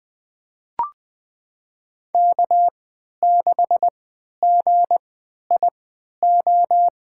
Morse code, heard as a keyed tone, about 20 wpm, spaced as for 11 wpm, 700 Hz. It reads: K6GIO
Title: USA Call Signs - 3x Character Spacing - 20wpm